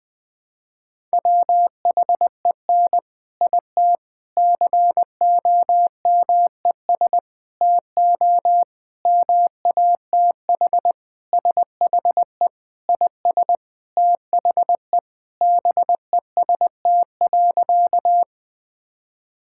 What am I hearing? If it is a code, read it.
Morse code, heard as a keyed tone, about 21 wpm, 700 Hz. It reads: WHEN IT COMES TO MATH SHE IS THE BEST.
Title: Sentences with 3-Word Phrases - 20wpm